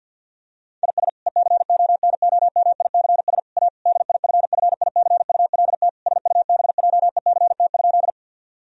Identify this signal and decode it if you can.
Morse code, heard as a keyed tone, about 50 wpm, 700 Hz. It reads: IS ECONOMICS A DIFFICULT SUBJECT?
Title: Sentences from Top 1000 Words - Review Part 3 - 50wpm